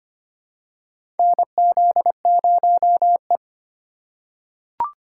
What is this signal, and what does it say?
Morse code, heard as a keyed tone, about 25 wpm, 700 Hz. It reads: NZ0E
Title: USA Call Signs - 4 Characters - 4x Character Spacing - 25wpm